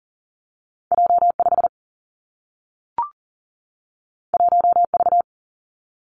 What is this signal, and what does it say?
Morse code, heard as a keyed tone, about 41 wpm, 700 Hz. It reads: J5 1V
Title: Two Challenging Letter-Number Combinations 40wpm